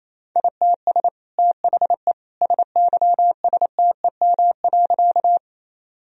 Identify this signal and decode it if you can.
Morse code, heard as a keyed tone, about 28 wpm, 700 Hz. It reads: ITS THE SYSTEM.
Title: Sentences from Top 1000 Words - Review Part 3 - 28wpm